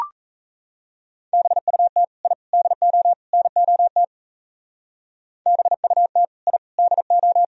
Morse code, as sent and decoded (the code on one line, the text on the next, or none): -... ..- - / .. / -.. --- / -. --- - / -... ..- - / .. / -.. ---
BUT I DO NOT BUT I DO